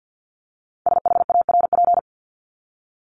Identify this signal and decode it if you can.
Morse code, heard as a keyed tone, about 50 wpm, 700 Hz. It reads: SHARP
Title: Top 1501-1750 Words 50wpm